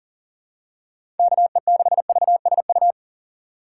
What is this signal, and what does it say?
Morse code, heard as a keyed tone, about 40 wpm, 700 Hz. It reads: KE6VSU
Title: USA Call Signs - 6 Characters - 2x Character Spacing - 40wpm